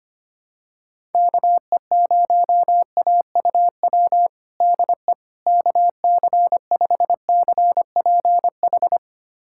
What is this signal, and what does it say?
Morse code, heard as a keyed tone, about 25 wpm, 700 Hz. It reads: KE0AUW DE KC5CPH
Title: USA Callsign DE Callsign - 6 Characters - 25wpm